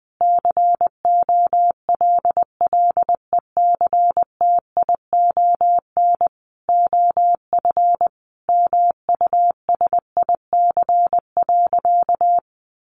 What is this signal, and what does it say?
Morse code, heard as a keyed tone, about 20 wpm, 700 Hz. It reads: COLLECTION OF MUSIC.
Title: Sentences from Top 1200 Words 20wpm